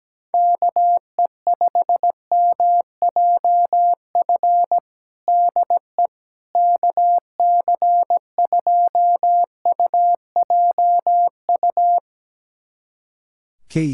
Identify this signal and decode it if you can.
Morse code, heard as a keyed tone, about 17 wpm, 700 Hz. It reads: KE5MJF DE KC2UJU
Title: USA Callsign DE Callsign - 6 Characters - 17wpm